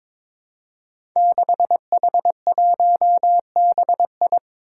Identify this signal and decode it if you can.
Morse code, heard as a keyed tone, about 22 wpm, 700 Hz. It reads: 6H1BI
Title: Five Challenging Letter-Number Combinations 22wpm